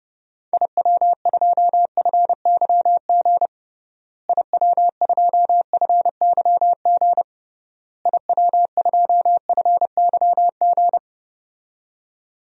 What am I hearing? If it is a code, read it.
Morse code, heard as a keyed tone, about 30 wpm, 700 Hz. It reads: IW2FYG IW2FYG IW2FYG
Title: DXCC-100 Call Signs - 6 Characters - Repeated 3x - 30wpm